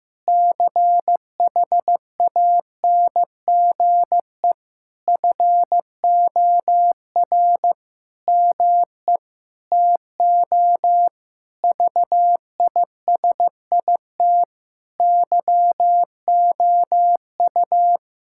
Morse code, heard as a keyed tone, about 16 wpm, 700 Hz. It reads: CHANGE FOR ME TO VISIT YOU
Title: Sentences from Top 1000 Words - Review Part 6 - 15wpm